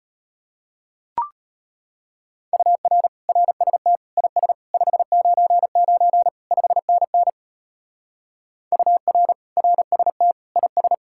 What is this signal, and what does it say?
Morse code, heard as a keyed tone, about 32 wpm, 700 Hz. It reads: UR RST IS 599 5NN UR RST IS